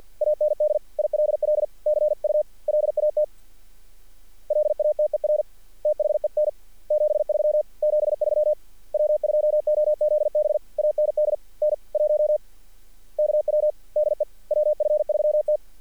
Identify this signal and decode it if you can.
Morse code, heard as a keyed tone, about 46 wpm, 610 Hz. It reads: AND ILL CU LAT LATER THEN 73 73 W2OZB AND N1KW DE WF3T
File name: archived / CRSnet